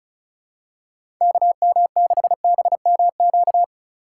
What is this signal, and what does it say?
Morse code, heard as a keyed tone, about 36 wpm, 700 Hz. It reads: KM6BMQ
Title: USA Call Signs - 6 Characters - 4x Character Spacing - 35wpm